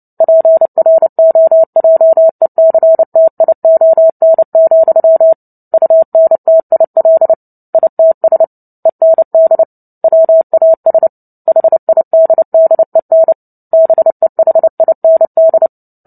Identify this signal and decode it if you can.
Morse code, heard as a keyed tone, about 30 wpm, 650 Hz. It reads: PROJECTION, UNTIL ITS END WAS HIDDEN BEHIND
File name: Book Two / 03-Chapter 3 - 29 WPM